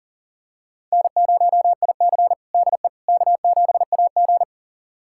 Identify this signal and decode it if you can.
Morse code, heard as a keyed tone, about 40 wpm, 700 Hz. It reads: N0IC DE K7AG
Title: USA Callsign DE Callsign - 4 Characters - 40wpm